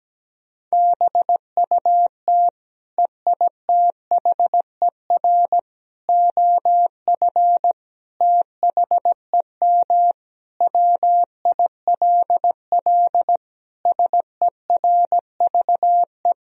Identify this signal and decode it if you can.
Morse code, heard as a keyed tone, about 17 wpm, 700 Hz. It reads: BUT EITHER OF THEM WILL SERVE
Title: Sentences from Top 1000 Words - Review Part 4 - 17wpm